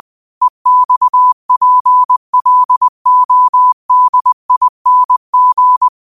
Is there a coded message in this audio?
Yes: Morse code from 0.4 to 5.9 s, heard as a keyed tone, about 20 wpm, 1000 Hz. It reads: EXPLODING